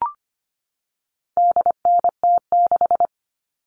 Morse code, heard as a keyed tone, about 25 wpm, 700 Hz. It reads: DNT6
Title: All Cut Numbers - 4-Digit 25wpm